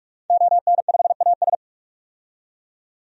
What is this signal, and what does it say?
Morse code, heard as a keyed tone, about 41 wpm, 700 Hz. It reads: ON5AS
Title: DXCC-100 Call Signs - 5 Characters - 45wpm